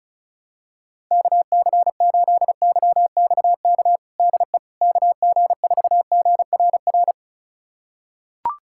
Morse code, heard as a keyed tone, about 36 wpm, 700 Hz. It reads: KC8YXK DE KG4GRR
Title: USA Callsign DE Callsign - 6 Characters - 35wpm